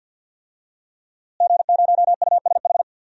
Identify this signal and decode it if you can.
Morse code, heard as a keyed tone, about 48 wpm, 700 Hz. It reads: G0USH
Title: DXCC-100 Call Signs - 5 Characters - Repeated 2x - 50wpm